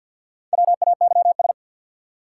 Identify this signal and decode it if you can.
Morse code, heard as a keyed tone, about 50 wpm, 700 Hz. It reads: WAYS
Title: Sets of 5 Words Encore - New Word Drill - 50wpm